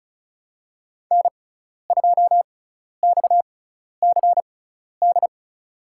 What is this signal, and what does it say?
Morse code, heard as a keyed tone, about 34 wpm, spaced as for 12 wpm, 700 Hz. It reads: N2XCD
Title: USA Call Signs - 6x Character Spacing - 35wpm